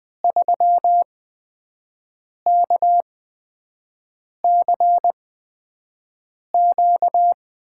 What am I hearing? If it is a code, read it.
Morse code, heard as a keyed tone, about 21 wpm, spaced as for 5 wpm, 700 Hz. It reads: 3KCQ
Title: USA Call Signs - 5 Characters - 8x Character Spacing - 20wpm